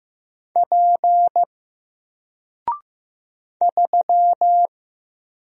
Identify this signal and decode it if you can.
Morse code, heard as a keyed tone, about 15 wpm, 700 Hz. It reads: P 3
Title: Single Character (Letter:Number:Punctuation:Prosign) 15wpm